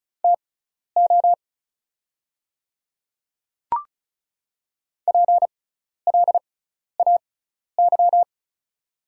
Morse code, heard as a keyed tone, about 36 wpm, spaced as for 12 wpm, 700 Hz. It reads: TO PLAY
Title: 6x Character Spacing - Top 300 Words - 35wpm